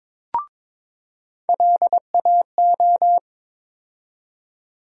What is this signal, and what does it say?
Morse code, heard as a keyed tone, about 23 wpm, 700 Hz. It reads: LAO
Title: USA Call Sign Postfixes - 3 Letters 22wpm